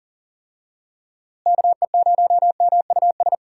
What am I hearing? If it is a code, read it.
Morse code, heard as a keyed tone, about 40 wpm, 700 Hz. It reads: KE0MUS
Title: USA Call Signs - 6 Characters - Repeated 3x - 40wpm